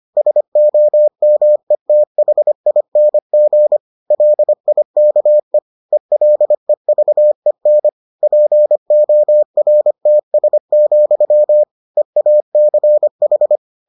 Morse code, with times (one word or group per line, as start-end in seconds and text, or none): 0.2-3.8 SOMETHING
4.1-5.6 LIKE
5.9-7.9 ELEVEN
8.2-11.6 PORTS,
12.0-13.6 EACH